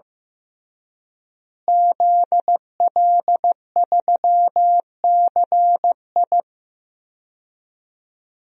Morse code, heard as a keyed tone, about 15 wpm, 700 Hz. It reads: ZL3CI
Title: DXCC-100 Call Signs - 5 Characters - Repeated 2x - 15wpm